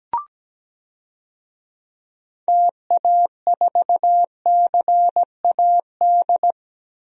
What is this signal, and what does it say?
Morse code, heard as a keyed tone, about 17 wpm, 700 Hz. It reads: TA4CAD
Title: DXCC-100 Call Signs - 6 Characters - 17wpm